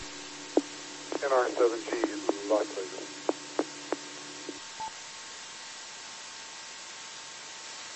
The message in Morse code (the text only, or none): none